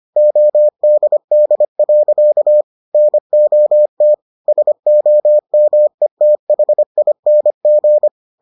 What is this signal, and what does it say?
Morse code, heard as a keyed tone, about 25 wpm, 600 Hz. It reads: ODD. NOT SOMETHING